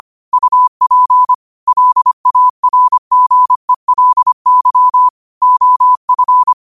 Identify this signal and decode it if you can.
Morse code, heard as a keyed tone, about 25 wpm, 1000 Hz. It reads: UP LARGELY OF
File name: qotdfiles / 25wpm